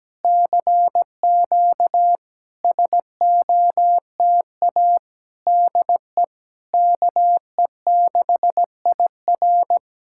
Morse code, heard as a keyed tone, about 17 wpm, 700 Hz. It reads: CQ SOTA DE KE6IR
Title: SOTA QSO Practice 17wpm